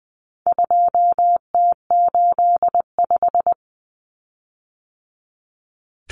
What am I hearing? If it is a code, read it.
Morse code, heard as a keyed tone, about 20 wpm, 700 Hz. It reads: 2T85
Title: Common Cut Numbers - 4-Digit 20wpm